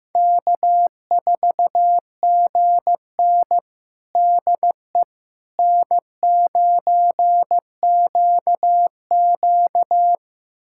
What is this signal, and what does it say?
Morse code, heard as a keyed tone, about 15 wpm, 700 Hz. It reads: K4GN DE N9QQ
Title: USA Callsign DE Callsign - 4 Characters - 15wpm